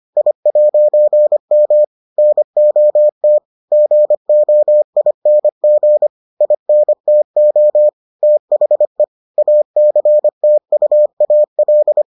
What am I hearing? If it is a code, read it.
Morse code, heard as a keyed tone, about 25 wpm, 600 Hz. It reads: I'M NOT GOING INTO THE ACTUAL